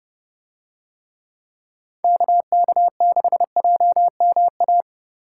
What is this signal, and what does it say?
Morse code, heard as a keyed tone, about 29 wpm, 700 Hz. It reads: KK6JMA